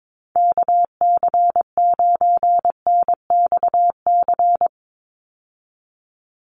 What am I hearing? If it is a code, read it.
Morse code, heard as a keyed tone, about 22 wpm, 700 Hz. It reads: KC9NXC
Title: USA Call Signs 22wpm